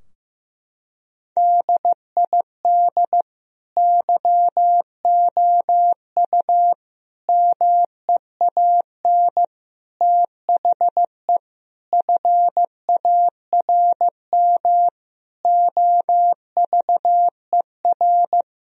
Morse code, heard as a keyed tone, about 15 wpm, 700 Hz. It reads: DID YOU MEAN THE FARM OVER
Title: Sentences from Top 1000 Words - Review Part 2 - 15wpm